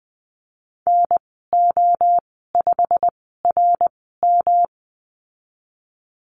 Morse code, heard as a keyed tone, about 20 wpm, spaced as for 15 wpm, 700 Hz. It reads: NO5RM
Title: 154 Lesson 25g - Letter p - Call Signs 20wpm